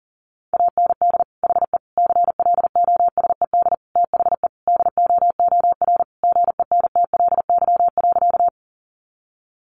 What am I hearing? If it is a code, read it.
Morse code, heard as a keyed tone, about 40 wpm, 700 Hz. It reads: AND HE CLOSED THE DOOR GENTLY.
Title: Sentences from Top 1500 Words 40wpm